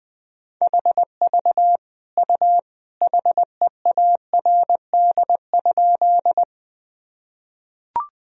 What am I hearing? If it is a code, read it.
Morse code, heard as a keyed tone, about 20 wpm, 700 Hz. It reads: HV U HEARD?
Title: Sentences from Top 400 Words and Abbreviations 20wpm